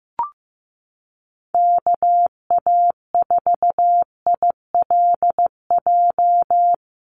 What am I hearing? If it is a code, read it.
Morse code, heard as a keyed tone, about 15 wpm, 700 Hz. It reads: KA4ILJ
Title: USA Call Signs 15wpm